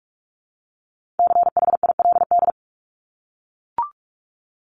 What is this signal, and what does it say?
Morse code, heard as a keyed tone, about 44 wpm, 700 Hz. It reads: CHILD
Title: Top 501-750 Words 45wpm